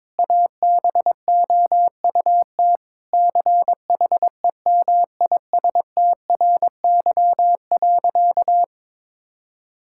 Morse code, heard as a keyed tone, about 22 wpm, 700 Hz. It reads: ABOUT CHEMISTRY.
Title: Sentences from Top 1000 Words - Review Part 6 - 22wpm